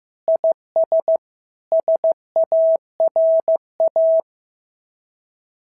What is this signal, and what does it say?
Morse code, heard as a keyed tone, about 14 wpm, 650 Hz. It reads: IS SARA
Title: SOTA QSO Practice 15wpm